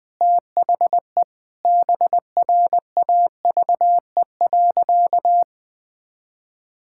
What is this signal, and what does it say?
Morse code, heard as a keyed tone, about 20 wpm, 700 Hz. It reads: THE BRAVE.
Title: Sentences with 9-Word Phrases - 20wpm